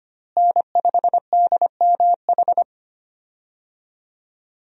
Morse code, heard as a keyed tone, about 25 wpm, 700 Hz. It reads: N5DMH
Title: USA Call Signs - 5 Characters - Repeated 2x - 25wpm